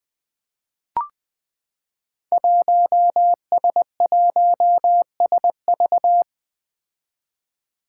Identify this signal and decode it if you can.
Morse code, heard as a keyed tone, about 20 wpm, 700 Hz. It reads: 1S1SV
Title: Five Challenging Letter-Number Combinations 20wpm